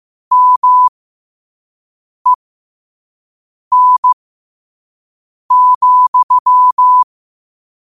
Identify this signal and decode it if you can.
Morse code, heard as a keyed tone, about 15 wpm, spaced as for 5 wpm, 1000 Hz. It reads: MEN,
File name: qotdfiles / 5wpm